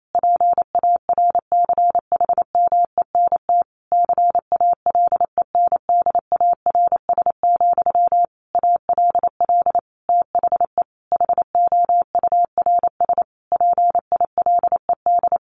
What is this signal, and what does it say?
Morse code, heard as a keyed tone, about 28 wpm, 700 Hz. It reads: PARCHMENT CALENDARS, ALL THE HOURS PILED
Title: A Sound Of Thunder 28wpm